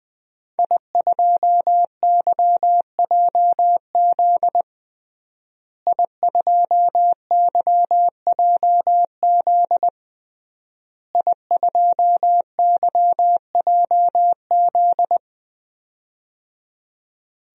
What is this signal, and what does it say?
Morse code, heard as a keyed tone, about 20 wpm, 700 Hz. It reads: I2YJZ I2YJZ I2YJZ
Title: DXCC-100 Call Signs - 5 Characters - Repeated 3x - 20wpm